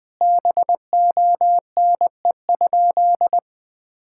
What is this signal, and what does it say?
Morse code, heard as a keyed tone, about 20 wpm, 700 Hz. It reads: BONE?
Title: Sentences from Top 1600 Words 20wpm